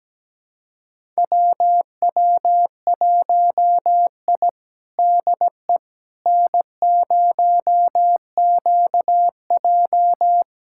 Morse code, heard as a keyed tone, about 17 wpm, 700 Hz. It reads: WW1I DE N0QJ
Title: USA Callsign DE Callsign - 4 Characters - 17wpm